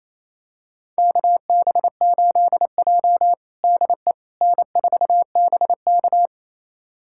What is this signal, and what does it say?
Morse code, heard as a keyed tone, about 28 wpm, 700 Hz. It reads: KB8J DE N4BK
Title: USA Callsign DE Callsign - 4 Characters - 28wpm